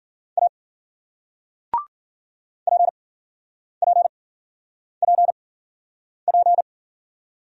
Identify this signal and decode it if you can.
Morse code, heard as a keyed tone, about 54 wpm, spaced as for 9 wpm, 700 Hz. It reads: A PPPP